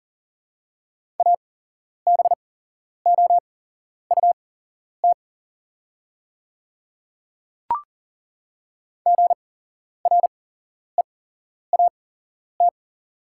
Morse code, heard as a keyed tone, about 40 wpm, spaced as for 11 wpm, 700 Hz. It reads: ABOUT GREAT
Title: 8x Character Spacing - Top 200 Words - 40wpm